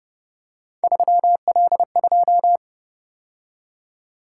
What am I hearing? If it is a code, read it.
Morse code, heard as a keyed tone, about 30 wpm, 700 Hz. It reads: 3L2